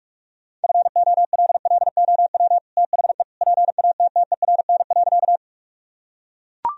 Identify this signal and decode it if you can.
Morse code, heard as a keyed tone, about 43 wpm, 700 Hz. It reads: FOLLOW THE PATTERN.
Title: Sentences from Top 1000 Words - Review Part 2 - 45wpm